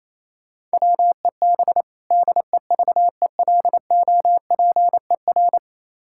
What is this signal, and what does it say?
Morse code, heard as a keyed tone, about 27 wpm, 700 Hz. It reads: WEB DEVELOPER
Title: K1USN SST QSO Practice 28wpm